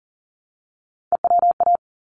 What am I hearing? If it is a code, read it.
Morse code, heard as a keyed tone, about 40 wpm, 700 Hz. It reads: EWA
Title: ARRL Sections 40wpm